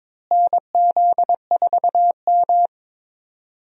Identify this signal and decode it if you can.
Morse code, heard as a keyed tone, about 22 wpm, 700 Hz. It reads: NZ4M
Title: USA Call Signs - 4 Characters - Repeated 2x - 22wpm